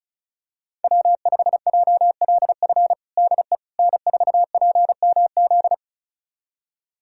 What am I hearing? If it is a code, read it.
Morse code, heard as a keyed tone, about 36 wpm, 700 Hz. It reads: W5JLF DE N4PMZ
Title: USA Callsign DE Callsign - 5 Characters - 35wpm